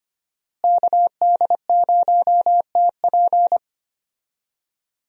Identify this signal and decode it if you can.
Morse code, heard as a keyed tone, about 25 wpm, 700 Hz. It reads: KD0TP
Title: USA Call Signs - 5 Characters - Repeated 3x - 25wpm